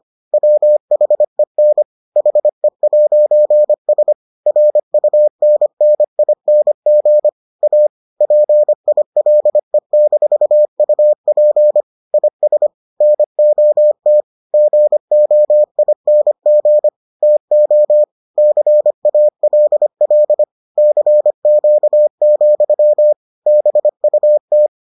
Morse code, heard as a keyed tone, about 25 wpm, 600 Hz. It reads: WHEN HE'S RUNNING A PILE-UP IS NOT GOING TO CALL CQ, BUT